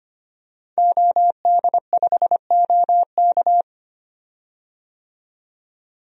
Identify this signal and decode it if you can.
Morse code, heard as a keyed tone, about 25 wpm, 700 Hz. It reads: OD5OK
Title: DXCC-100 Call Signs - 5 Characters - 25wpm